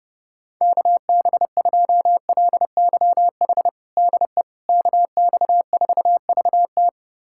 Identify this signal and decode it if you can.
Morse code, heard as a keyed tone, about 30 wpm, 700 Hz. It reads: KB2LYH DE KX4VT
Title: USA Callsign DE Callsign 30wpm